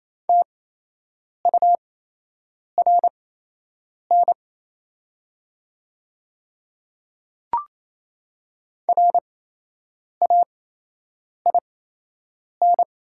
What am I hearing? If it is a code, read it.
Morse code, heard as a keyed tone, about 27 wpm, spaced as for 8 wpm, 700 Hz. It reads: TURN RAIN
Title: 8x Character Spacing - Top 300 Words - 28wpm